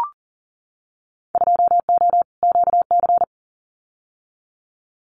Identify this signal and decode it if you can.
Morse code, heard as a keyed tone, about 40 wpm, 700 Hz. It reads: 2O QC
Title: Winter Field Day Exchanges 40wpm